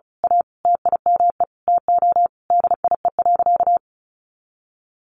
Morse code, heard as a keyed tone, about 34 wpm, 700 Hz. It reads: A TIME TO DIE.
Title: Sentences from Top 1300 Words 35wpm